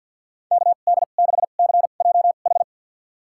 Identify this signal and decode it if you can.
Morse code, heard as a keyed tone, about 51 wpm, spaced as for 36 wpm, 700 Hz. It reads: KD6CJH
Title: USA Call Signs - 2x Character Spacing - 50wpm